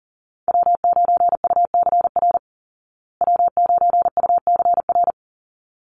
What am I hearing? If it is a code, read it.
Morse code, heard as a keyed tone, about 40 wpm, 700 Hz. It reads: W9UCR W9UCR
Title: Call Signs Repeated 3x - 40wpm